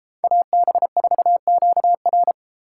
Morse code, heard as a keyed tone, about 32 wpm, 700 Hz. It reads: AB4QR